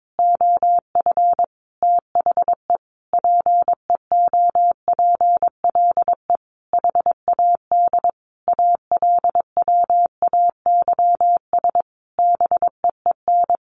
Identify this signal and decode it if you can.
Morse code, heard as a keyed tone, about 22 wpm, 700 Hz. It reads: OF THE PEOPLE HAD ALWAYS BEEN
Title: Sentences from Top 1400 Words 22wpm